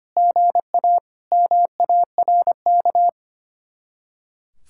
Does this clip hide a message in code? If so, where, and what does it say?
Yes: Morse code from 0.2 to 3.1 s, heard as a keyed tone, about 33 wpm, 700 Hz. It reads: GA MARK